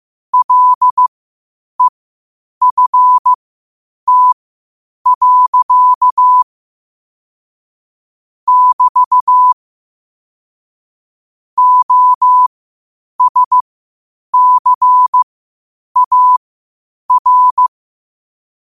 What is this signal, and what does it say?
Morse code, heard as a keyed tone, about 15 wpm, spaced as for 8 wpm, 1000 Hz. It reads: LEFT. <BT> OSCAR
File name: qotdfiles / 8wpm